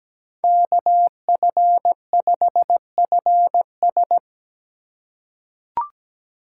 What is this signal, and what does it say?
Morse code, heard as a keyed tone, about 17 wpm, 700 Hz. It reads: KF5FS
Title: USA Call Signs - 5 Characters - 4x Character Spacing - 17wpm